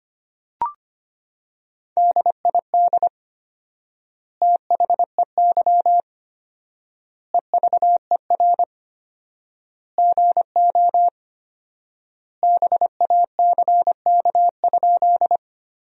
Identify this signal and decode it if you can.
Morse code, heard as a keyed tone, about 25 wpm, 700 Hz. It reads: DID THEY EVER GO BACK?
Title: 4x Word-Spacing - Sentences from Top 400 Words - 25wpm